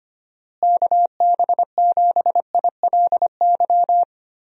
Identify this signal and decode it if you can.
Morse code, heard as a keyed tone, about 25 wpm, 700 Hz. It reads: KB7ILY
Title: USA Call Signs - 6 Characters - Repeated 3x - 25wpm